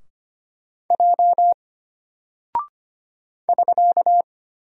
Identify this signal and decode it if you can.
Morse code, heard as a keyed tone, about 25 wpm, 700 Hz. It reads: J <SK>
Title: Single Character (Letter:Number:Punctuation:Prosign) Alt 25wpm